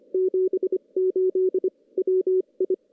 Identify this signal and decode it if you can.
Morse code, heard as a keyed tone, about 25 wpm, 380 Hz. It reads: 78 WI
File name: CW / CW-SS-k5tr- -1700